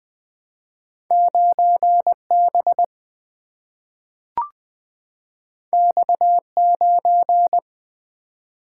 Morse code, heard as a keyed tone, about 20 wpm, 700 Hz. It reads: 9B X9
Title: Single-Letter Single-Number 20wpm